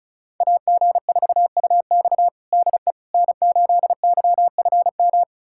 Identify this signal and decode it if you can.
Morse code, heard as a keyed tone, about 34 wpm, 700 Hz. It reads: AG4UX DE N8YFM
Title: USA Callsign DE Callsign - 5 Characters - 35wpm